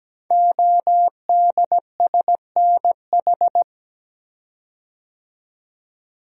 Morse code, heard as a keyed tone, about 17 wpm, 700 Hz. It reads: ODSNH
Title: Code Groups - 5 Letter:Numbers - 17wpm